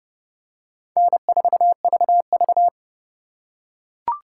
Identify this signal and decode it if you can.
Morse code, heard as a keyed tone, about 32 wpm, 700 Hz. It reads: N4VV
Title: USA Call Signs - 4 Characters - 4x Character Spacing - 30wpm